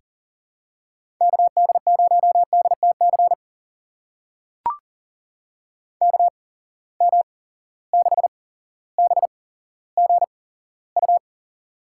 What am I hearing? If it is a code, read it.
Morse code, heard as a keyed tone, about 37 wpm, spaced as for 17 wpm, 700 Hz. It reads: KD0DTC KM6BGU
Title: USA Call Signs - 6 Characters - 8x Character Spacing - 40wpm